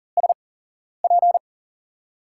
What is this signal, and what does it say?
Morse code, heard as a keyed tone, about 38 wpm, spaced as for 11 wpm, 700 Hz. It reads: SP
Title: USA Call Signs - 6 Characters - 8x Character Spacing - 40wpm